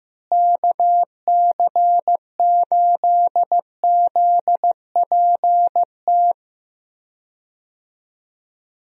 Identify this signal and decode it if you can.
Morse code, heard as a keyed tone, about 16 wpm, 700 Hz. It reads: KC8ZPT
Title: USA Call Signs - 6 Characters - Repeated 3x - 15wpm